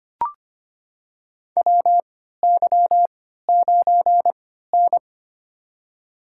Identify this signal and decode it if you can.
Morse code, heard as a keyed tone, about 25 wpm, spaced as for 14 wpm, 700 Hz. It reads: WY9N
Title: USA Call Signs - 3x Character Spacing - 25wpm